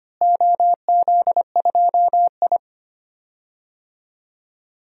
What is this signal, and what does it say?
Morse code, heard as a keyed tone, about 25 wpm, 700 Hz. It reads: OZ2I
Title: DXCC-100 Call Signs - 4 Characters - Repeated 3x - 25wpm